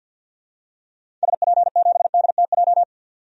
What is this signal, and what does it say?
Morse code, heard as a keyed tone, about 51 wpm, 700 Hz. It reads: SP7DTJ
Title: DXCC-100 Call Signs - 6 Characters - 50wpm